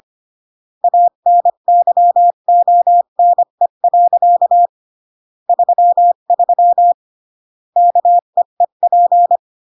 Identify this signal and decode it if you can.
Morse code, heard as a keyed tone, about 25 wpm, 700 Hz. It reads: ANYONE. 33 KEEP
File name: CW / jodorowsky / 82 commandments 25-20